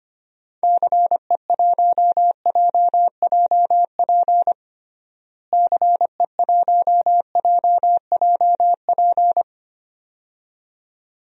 Morse code, heard as a keyed tone, about 25 wpm, 700 Hz. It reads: CE1JJP CE1JJP
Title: DXCC-100 Call Signs - 6 Characters - Repeated 2x - 25wpm